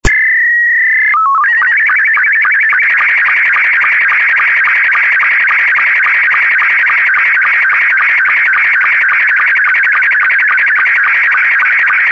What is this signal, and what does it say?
SSTV (slow-scan television), Scottie 2.
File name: sstv / sstv2 / scottie s2